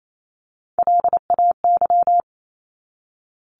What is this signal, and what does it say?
Morse code, heard as a keyed tone, about 28 wpm, 700 Hz. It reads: LAY
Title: Top 500 Words 28wpm